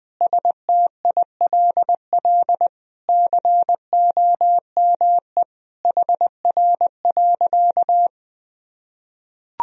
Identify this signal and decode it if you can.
Morse code, heard as a keyed tone, about 21 wpm, 700 Hz. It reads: STILL COME HR.
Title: Sentences from Top 300 Words and Abbreviations 20wpm